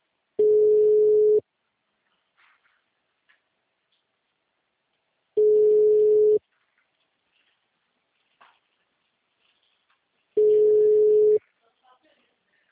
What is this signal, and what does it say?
Ringback tone (Europe, 425 Hz), with speech.
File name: out- -2004- - - .4